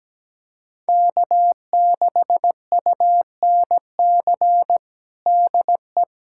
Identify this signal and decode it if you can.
Morse code, heard as a keyed tone, about 17 wpm, 700 Hz. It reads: K6UNC DE
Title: USA Callsign DE Callsign - 5 Characters - 17wpm